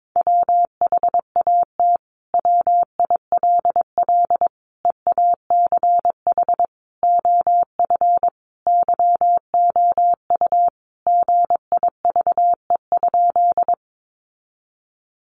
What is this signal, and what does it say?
Morse code, heard as a keyed tone, about 22 wpm, 700 Hz. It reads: WHAT WILL EACH OF YOU GIVE?
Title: Sentences from Top 200 Words 22wpm